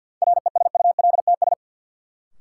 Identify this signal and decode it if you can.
Morse code, heard as a keyed tone, about 53 wpm, 700 Hz. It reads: RESULTS
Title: Sets of 4 Words - New Word Drill - 50wpm